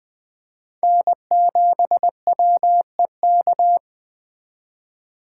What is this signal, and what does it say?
Morse code, heard as a keyed tone, about 23 wpm, 700 Hz. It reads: N7WEK
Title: Speed Racing - USA 1x3 Call Signs 20wpm